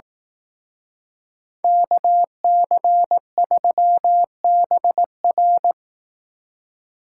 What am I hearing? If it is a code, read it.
Morse code, heard as a keyed tone, about 18 wpm, 700 Hz. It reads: KC3BR
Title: Speed Racing - USA Call Signs 15wpm